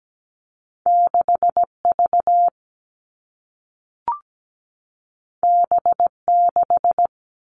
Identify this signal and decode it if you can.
Morse code, heard as a keyed tone, about 17 wpm, 700 Hz. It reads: 6V B6
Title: Two Challenging Letter-Number Combinations 17wpm